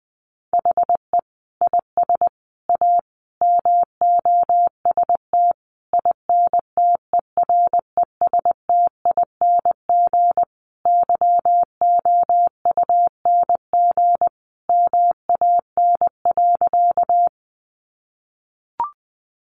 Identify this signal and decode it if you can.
Morse code, heard as a keyed tone, about 20 wpm, 700 Hz. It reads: HE IS A MOST INTERESTING YOUNG MAN.
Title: Sentences from Top 1000 Words 20wpm